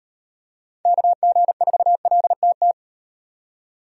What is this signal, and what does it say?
Morse code, heard as a keyed tone, about 34 wpm, 700 Hz. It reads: KG4LTT